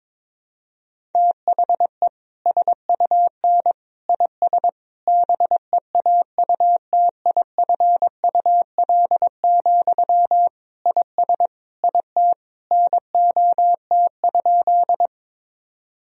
Morse code, heard as a keyed tone, about 22 wpm, 700 Hz. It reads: THE SUN IS BEAUTIFUL, IS IT NOT?
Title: Sentences from Top 1000 Words - Review Part 1 - 22wpm